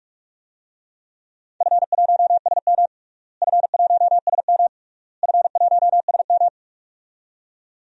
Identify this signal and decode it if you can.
Morse code, heard as a keyed tone, about 41 wpm, 700 Hz. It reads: F1SM F1SM F1SM
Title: DXCC-100 Call Signs - 4 Characters - Repeated 3x - 45wpm